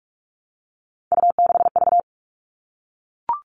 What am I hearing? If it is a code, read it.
Morse code, heard as a keyed tone, about 44 wpm, 700 Hz. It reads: U6V
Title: All Cut Numbers - 3-Digit 45wpm